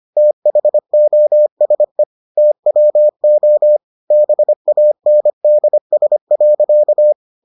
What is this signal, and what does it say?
Morse code, heard as a keyed tone, about 25 wpm, 600 Hz. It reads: THOSE TWO BANDS.